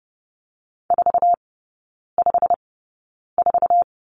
Morse code, heard as a keyed tone, about 30 wpm, spaced as for 9 wpm, 700 Hz. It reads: 454
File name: Lesson 18e - Letter m - Three Characters 30wpm